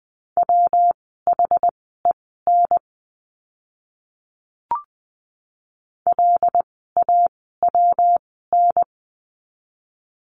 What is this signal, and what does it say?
Morse code, heard as a keyed tone, about 20 wpm, spaced as for 15 wpm, 700 Hz. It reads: WHEN LAWN
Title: 115 Lesson 19f - Letter w - Words 20wpm